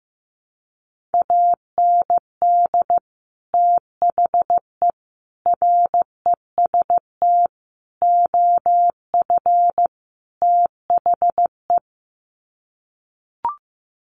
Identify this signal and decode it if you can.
Morse code, heard as a keyed tone, about 15 wpm, 700 Hz. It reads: AND THE REST OF THE
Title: Sets of 5 Words 15wpm